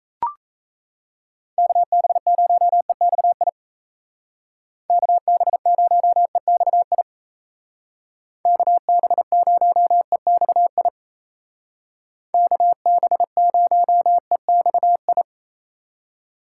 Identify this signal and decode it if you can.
Morse code, heard as a keyed tone, about 35 wpm, 700 Hz. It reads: KB0EXI KB0EXI KB0EXI KB0EXI
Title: Speed Racing - USA 2x3 Call Signs 28wpm